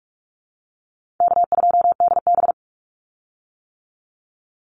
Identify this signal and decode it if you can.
Morse code, heard as a keyed tone, about 44 wpm, 700 Hz. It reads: K2DB
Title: Easy Call Signs - 45wpm